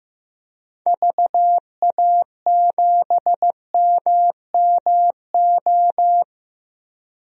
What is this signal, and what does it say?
Morse code, heard as a keyed tone, about 16 wpm, 700 Hz. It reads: VA7MMO
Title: DXCC-100 Call Signs - 6 Characters - 15wpm